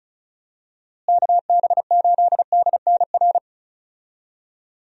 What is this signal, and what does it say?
Morse code, heard as a keyed tone, about 36 wpm, 700 Hz. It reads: KB8DNR
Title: USA Call Signs - 6 Characters - Repeated 2x - 35wpm